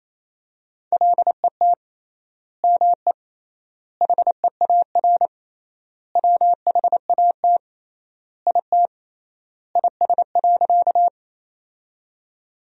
Morse code, heard as a keyed tone, about 28 wpm, 700 Hz. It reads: LET ME HEAR WHAT IT IS.
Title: 3x Word-Spacing - Sentences from Top 300 Words - 28wpm